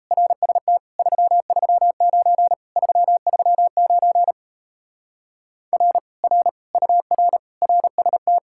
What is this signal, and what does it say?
Morse code, heard as a keyed tone, about 37 wpm, 700 Hz. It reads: RST 339 339 R R UR RST